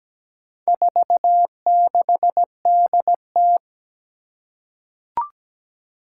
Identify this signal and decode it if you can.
Morse code, heard as a keyed tone, about 17 wpm, 700 Hz. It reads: 46DT
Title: Four Challenging Letter-Number Combinations 17wpm